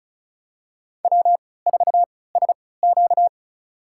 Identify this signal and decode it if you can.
Morse code, heard as a keyed tone, about 36 wpm, spaced as for 20 wpm, 700 Hz. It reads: W4SQ
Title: USA Call Signs - 4 Characters - 3x Character Spacing - 35wpm